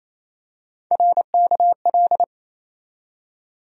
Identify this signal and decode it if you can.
Morse code, heard as a keyed tone, about 27 wpm, 700 Hz. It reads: RKL
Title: USA Call Sign Postfixes - 3 Letters 28wpm